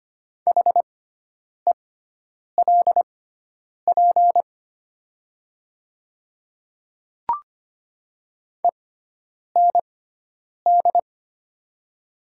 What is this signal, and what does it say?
Morse code, heard as a keyed tone, about 25 wpm, spaced as for 9 wpm, 700 Hz. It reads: HELP END
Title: 6x Character Spacing - Top 200 Words - 25wpm